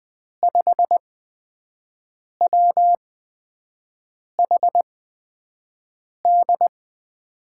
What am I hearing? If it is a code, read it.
Morse code, heard as a keyed tone, about 20 wpm, spaced as for 5 wpm, 700 Hz. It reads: 5WHD
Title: USA Call Signs - 6 Characters - 8x Character Spacing - 20wpm